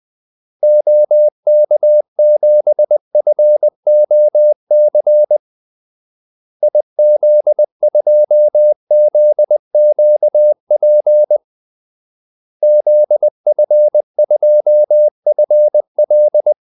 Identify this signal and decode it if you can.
Morse code, heard as a keyed tone, about 20 wpm, 600 Hz. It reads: OK7FOC IZ2ZQP ZF2FL
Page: Kaldesignaler 20 wpm